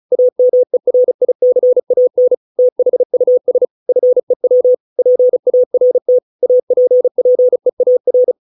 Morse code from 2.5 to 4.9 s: THUS FEW